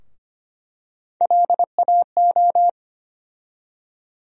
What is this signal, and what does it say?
Morse code, heard as a keyed tone, about 25 wpm, 700 Hz. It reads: LAO
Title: USA Call Sign Postfixes - 3 Letters 25wpm